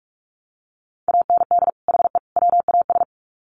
Morse code, heard as a keyed tone, about 44 wpm, 700 Hz. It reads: AND HE WAS
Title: Sets of 3 Words 45wpm